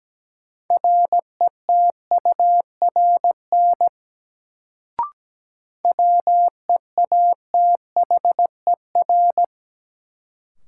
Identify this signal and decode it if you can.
Morse code, heard as a keyed tone, about 17 wpm, 700 Hz. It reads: RETURN WEATHER
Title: Sentences from Top 800 Words - New Word Drill - 17wpm